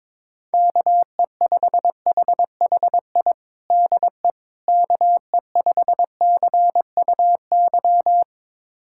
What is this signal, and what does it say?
Morse code, heard as a keyed tone, about 22 wpm, 700 Hz. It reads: KE5HHI DE KE5CUY
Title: USA Callsign DE Callsign - 6 Characters - 22wpm